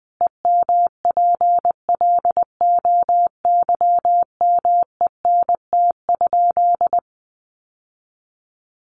Morse code, heard as a keyed tone, about 19 wpm, 700 Hz. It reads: EMPLOYMENT?
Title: Sentences from Top 1500 Words 20wpm